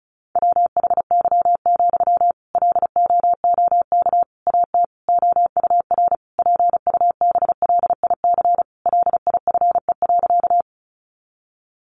Morse code, heard as a keyed tone, about 34 wpm, 700 Hz. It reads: WHY, LOOK AT OUR PUBLIC LIFE.
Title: Sentences from Top 600 Words 35wpm